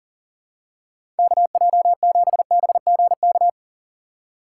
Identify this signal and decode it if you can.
Morse code, heard as a keyed tone, about 40 wpm, 700 Hz. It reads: KJ7BGK
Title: USA Call Signs - 6 Characters - 6x Character Spacing - 40wpm